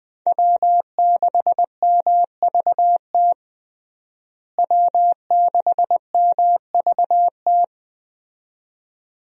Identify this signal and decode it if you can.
Morse code, heard as a keyed tone, about 20 wpm, 700 Hz. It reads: W6MVT W6MVT
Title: USA Call Signs - 5 Characters - Repeated 2x - 20wpm